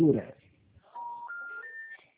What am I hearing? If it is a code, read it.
Special information tone, with speech.
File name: out- -2027- - - .11